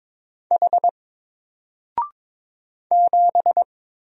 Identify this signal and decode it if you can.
Morse code, heard as a keyed tone, about 22 wpm, 700 Hz. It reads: H 7
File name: Single-Letter-Number-22wpm